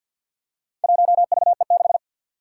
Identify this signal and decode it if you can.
Morse code, heard as a keyed tone, about 47 wpm, 700 Hz. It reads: 1VE6
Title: Four Challenging Letter-Number Combinations 50wpm